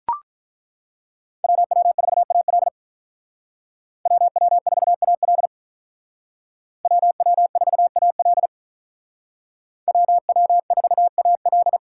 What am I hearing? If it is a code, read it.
Morse code, heard as a keyed tone, about 42 wpm, 700 Hz. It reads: WW4AL WW4AL WW4AL WW4AL
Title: Speed Racing - USA 2x2 Call Signs 35wpm